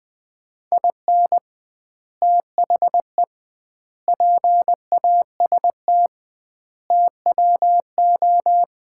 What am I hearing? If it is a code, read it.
Morse code, heard as a keyed tone, about 20 wpm, 700 Hz. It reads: IN THE PAST TWO
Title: 2x Word-Spacing - Sets of 5 Words - 20wpm